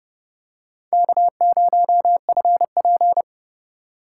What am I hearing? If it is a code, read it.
Morse code, heard as a keyed tone, about 28 wpm, 700 Hz. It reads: K0FP